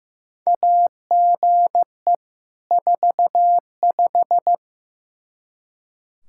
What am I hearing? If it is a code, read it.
Morse code, heard as a keyed tone, about 15 wpm, 700 Hz. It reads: AGE 45
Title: Practice QSO - Exchange 3 - More Info - 15wpm